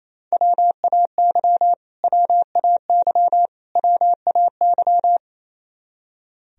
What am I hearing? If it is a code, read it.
Morse code, heard as a keyed tone, about 28 wpm, 700 Hz. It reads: WAY WAY WAY
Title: Top 100 Words Repeated 28wpm